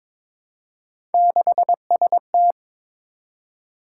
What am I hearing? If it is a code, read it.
Morse code, heard as a keyed tone, about 22 wpm, 700 Hz. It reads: 6ST
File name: practice / 22wpm / Three Letter-Number Combinations - 22wpm